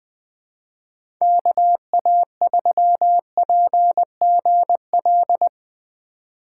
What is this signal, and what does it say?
Morse code, heard as a keyed tone, about 20 wpm, 700 Hz. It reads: KA3PGL
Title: USA Call Signs - 6 Characters - 4x Character Spacing - 20wpm